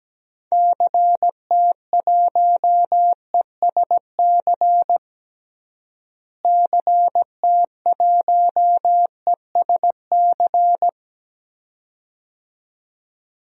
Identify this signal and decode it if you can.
Morse code, heard as a keyed tone, about 17 wpm, 700 Hz. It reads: CT1ESC CT1ESC
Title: DXCC-100 Call Signs - 6 Characters - Repeated 2x - 17wpm